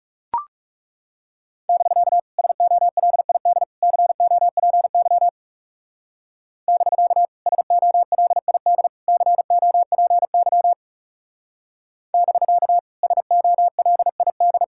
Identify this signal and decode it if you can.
Morse code, heard as a keyed tone, about 36 wpm, 700 Hz. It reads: <BK> SOLID COPY <BK> SOLID COPY <BK> SOLID
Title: Speed-Racing Common QSO Phrases - 30wpm